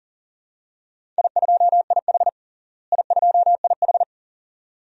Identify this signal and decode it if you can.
Morse code, heard as a keyed tone, about 40 wpm, 700 Hz. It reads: I2IH I2IH
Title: DXCC-100 Call Signs - 4 Characters - Repeated 2x - 40wpm